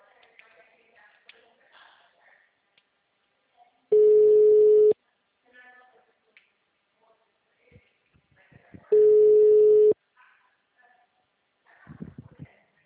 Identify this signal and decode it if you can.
Ringback tone (Europe, 425 Hz).